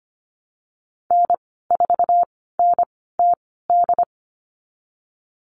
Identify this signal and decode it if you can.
Morse code, heard as a keyed tone, about 25 wpm, spaced as for 16 wpm, 700 Hz. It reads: N4NTD
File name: Lesson 15g - Number 5 - Call Signs 25wpm